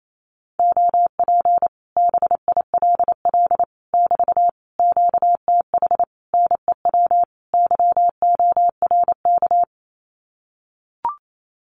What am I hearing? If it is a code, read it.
Morse code, heard as a keyed tone, about 28 wpm, 700 Hz. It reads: OP BILL <BT> QTH NEW YORK
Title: Common QSO Phrases Encore 28wpm